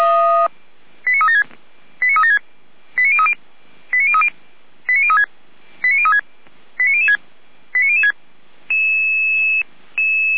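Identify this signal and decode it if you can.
Five-tone selective call (ZVEI), heard as quick runs of five tones, tones of about 70 ms. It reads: 89376 89376 89030 89030 89036 89036 89006 89006
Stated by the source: SDR-received ZVEI five-tone call 89001, from the audio server of the Orlamünde fire brigade (feuerwehr-orlamuende).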